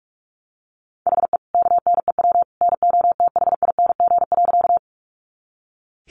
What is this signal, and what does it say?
Morse code, heard as a keyed tone, about 44 wpm, 700 Hz. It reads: HE KNEW NOTHING.
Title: Sentences from Top 500 Words 45wpm